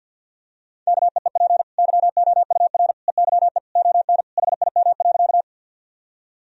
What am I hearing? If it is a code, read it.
Morse code, heard as a keyed tone, about 52 wpm, 700 Hz. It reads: KEEP YOUR EYE ON HIM.
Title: Sentences from Top 1000 Words - Review Part 1 - 50wpm